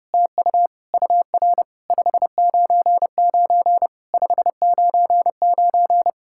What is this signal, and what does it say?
Morse code, heard as a keyed tone, about 29 wpm, 700 Hz. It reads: TU UR 599 599
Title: SOTA QSO Practice 30wpm